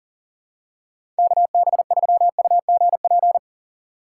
Morse code, heard as a keyed tone, about 40 wpm, 700 Hz. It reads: KB3UGP
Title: USA Call Signs - 6 Characters - 6x Character Spacing - 40wpm